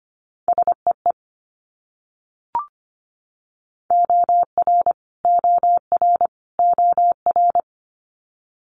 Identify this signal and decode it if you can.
Morse code, heard as a keyed tone, about 25 wpm, 700 Hz. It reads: SEE OR OR OR
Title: Multi-letter Phonemes Repeated 3x 25wpm